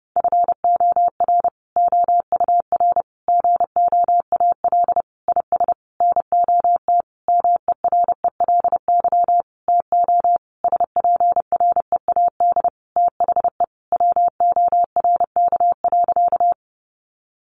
Morse code, heard as a keyed tone, about 30 wpm, 700 Hz. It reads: FOR OUR GOAL IS NOT MERELY TO SPREAD THE WORK.
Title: Sentences from Top 1400 Words 30wpm